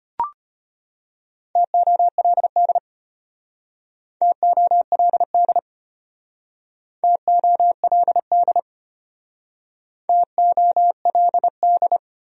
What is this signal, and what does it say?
Morse code, heard as a keyed tone, about 32 wpm, 700 Hz. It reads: TOLD TOLD TOLD TOLD